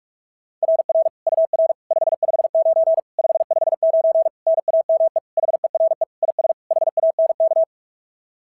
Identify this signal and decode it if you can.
Morse code, heard as a keyed tone, about 44 wpm, 660 Hz. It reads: RR UR 559 559 NAME HERE IS HANK
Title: SOTA QSO Practice 45wpm